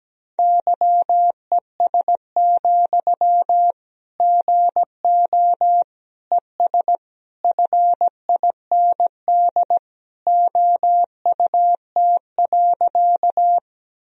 Morse code, heard as a keyed tone, about 17 wpm, 700 Hz. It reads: YES, GO ES FIND OUT.
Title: Sentences from Top 500 Words and Abbreviations 17wpm